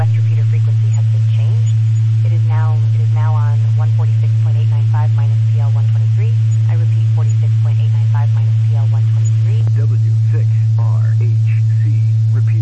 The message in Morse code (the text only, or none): none